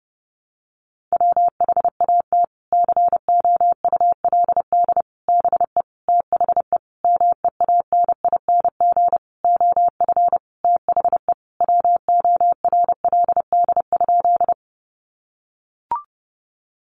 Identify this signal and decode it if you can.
Morse code, heard as a keyed tone, about 30 wpm, 700 Hz. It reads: WHAT COULD BE THE MEANING OF THE WORLD?
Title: Sentences from Top 900 Words 30wpm